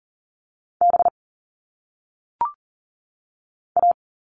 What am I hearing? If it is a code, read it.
Morse code, heard as a keyed tone, about 39 wpm, 700 Hz. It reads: B A
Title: All Cut Numbers - 1-Digit 40wpm